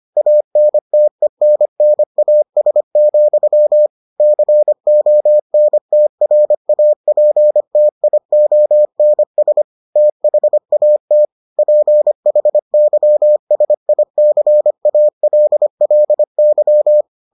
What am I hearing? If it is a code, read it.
Morse code, heard as a keyed tone, about 25 wpm, 600 Hz. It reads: ANTENNAS, CONTRAPTIONS THAT PHYSICALLY